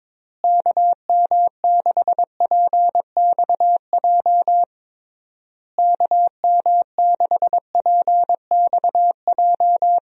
Morse code, heard as a keyed tone, about 22 wpm, 700 Hz. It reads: KM6PXJ KM6PXJ
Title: USA Call Signs - 6 Characters - Repeated 3x - 22wpm